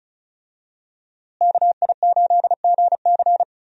Morse code, heard as a keyed tone, about 36 wpm, 700 Hz. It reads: KI8GC
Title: USA Call Signs - 5 Characters - Repeated 3x - 35wpm